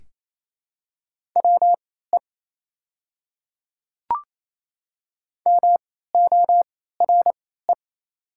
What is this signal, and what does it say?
Morse code, heard as a keyed tone, about 27 wpm, spaced as for 16 wpm, 690 Hz. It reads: WE MORE
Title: 3x Character Spacing - Top 500 Words - 28wpm